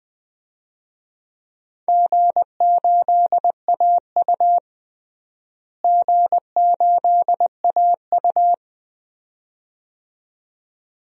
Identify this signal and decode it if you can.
Morse code, heard as a keyed tone, about 20 wpm, 700 Hz. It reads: G8AU G8AU
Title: DXCC-100 Call Signs - 4 Characters - Repeated 2x - 20wpm